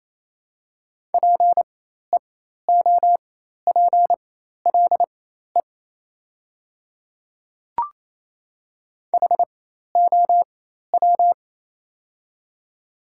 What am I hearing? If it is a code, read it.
Morse code, heard as a keyed tone, about 28 wpm, spaced as for 13 wpm, 700 Hz. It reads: PEOPLE HOW
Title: 4x Character Spacing - Top 100 Words - 28wpm